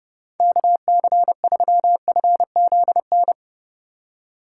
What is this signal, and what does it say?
Morse code, heard as a keyed tone, about 32 wpm, 700 Hz. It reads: KC3FZN